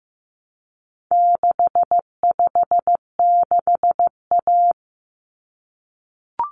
Morse code, heard as a keyed tone, about 15 wpm, 700 Hz. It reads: 656A
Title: Common Cut Numbers - 4-Digit 15wpm